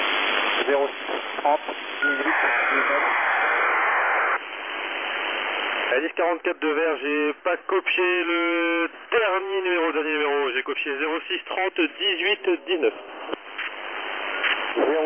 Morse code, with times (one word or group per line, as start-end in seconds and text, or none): none